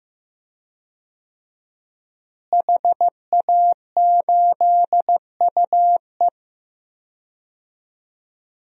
Morse code, heard as a keyed tone, about 15 wpm, 700 Hz. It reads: HA8UE
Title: DXCC-100 Call Signs - 5 Characters - Repeated 2x - 15wpm